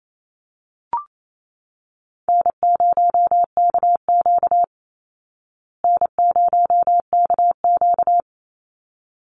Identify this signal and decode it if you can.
Morse code, heard as a keyed tone, about 28 wpm, 700 Hz. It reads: N0KQ N0KQ
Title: Easy Call Signs Repeated 2x - 28wpm